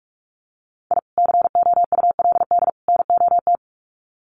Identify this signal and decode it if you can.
Morse code, heard as a keyed tone, about 44 wpm, 700 Hz. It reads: I COULD NOT
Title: Sets of 3 Words 45wpm